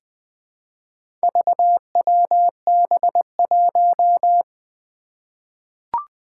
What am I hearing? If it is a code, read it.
Morse code, heard as a keyed tone, about 20 wpm, 700 Hz. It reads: VWB1
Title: Four Challenging Letter-Number Combinations 20wpm